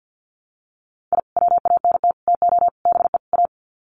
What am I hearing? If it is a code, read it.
Morse code, heard as a keyed tone, about 50 wpm, 700 Hz. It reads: I WANT TO BE A
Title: Sets of 5 Words Encore 50wpm